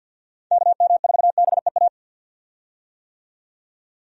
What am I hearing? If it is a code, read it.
Morse code, heard as a keyed tone, about 51 wpm, 700 Hz. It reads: KM4BEA
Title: USA Call Signs - 6 Characters - 50wpm